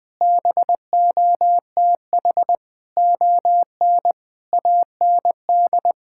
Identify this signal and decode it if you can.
Morse code, heard as a keyed tone, about 20 wpm, 700 Hz. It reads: BOTH ON AND
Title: Sentences with 6-Word Phrases - 20wpm